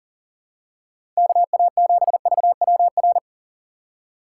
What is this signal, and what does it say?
Morse code, heard as a keyed tone, about 40 wpm, 700 Hz. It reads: KA7VWR
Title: USA Call Signs - 6 Characters - 4x Character Spacing - 40wpm